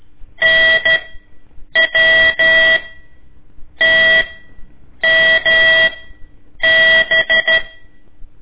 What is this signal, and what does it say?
Morse code, heard as a keyed tone, about 13 wpm, spaced as for 7 wpm, 1950 Hz. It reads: NWTMB
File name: ik1weg2 / CORSI CW / lez4-1